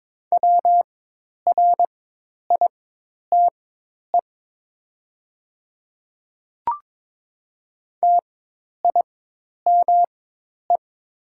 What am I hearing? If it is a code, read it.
Morse code, heard as a keyed tone, about 21 wpm, spaced as for 10 wpm, 700 Hz. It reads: WRITE TIME
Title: 4x Character Spacing - Top 100 Words - 22wpm